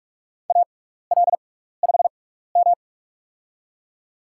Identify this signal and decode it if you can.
Morse code, heard as a keyed tone, about 46 wpm, spaced as for 15 wpm, 700 Hz. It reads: AL5M
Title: USA Call Signs - 4 Characters - 6x Character Spacing - 45wpm